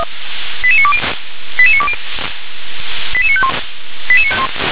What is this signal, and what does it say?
Five-tone selective call (ZVEI), heard as quick runs of five tones, tones of about 70 ms. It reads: 88920 88920 88951
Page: ZVEI